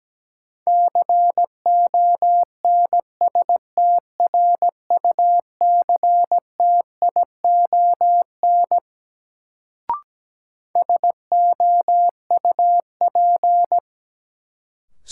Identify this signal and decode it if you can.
Morse code, heard as a keyed tone, about 17 wpm, 700 Hz. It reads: CONSTRUCTION SOUP
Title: Sentences from Top 1600 Words - New Word Drill Alt - 17wpm